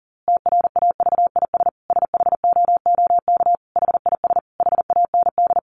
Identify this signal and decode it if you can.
Morse code, heard as a keyed tone, about 40 wpm, 700 Hz. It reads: TRAVIS SHOOK HIS HAND
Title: A Sound Of Thunder 40wpm